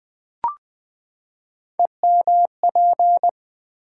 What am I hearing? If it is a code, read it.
Morse code, heard as a keyed tone, about 20 wpm, 700 Hz. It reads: EMP
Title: Build A Word - 3 Letters At A Time - 8-Letter Words - 20wpm